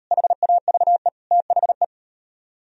Morse code, heard as a keyed tone, about 38 wpm, 700 Hz. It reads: HAVE THE